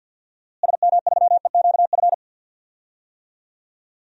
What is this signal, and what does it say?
Morse code, heard as a keyed tone, about 51 wpm, 700 Hz. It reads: SM3EQF
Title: DXCC-100 Call Signs - 6 Characters - Repeated 3x - 50wpm